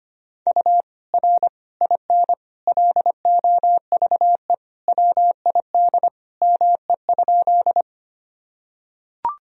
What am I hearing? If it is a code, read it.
Morse code, heard as a keyed tone, about 25 wpm, 700 Hz. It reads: U R IN LOVE WID ME?
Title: Sentences from Top 400 Words and Abbreviations 25wpm